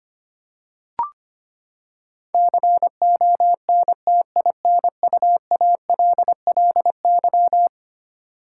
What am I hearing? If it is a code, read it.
Morse code, heard as a keyed tone, about 25 wpm, 700 Hz. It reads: CONTINUALLY
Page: morse-code-ninja-practice-25wpm